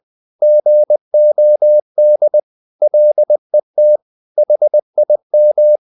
Morse code, heard as a keyed tone, about 20 wpm, 600 Hz. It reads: GOD LET HIM
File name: the old man and the sea / the old man and the sea part 1 20wpm